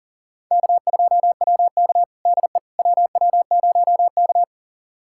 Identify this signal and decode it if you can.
Morse code, heard as a keyed tone, about 40 wpm, 700 Hz. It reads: K2WK DE WW0K
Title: USA Callsign DE Callsign - 4 Characters - 40wpm